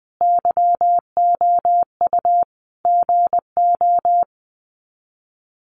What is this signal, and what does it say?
Morse code, heard as a keyed tone, about 20 wpm, 700 Hz. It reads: YOU GO
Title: Sets of 5 Words Encore 20wpm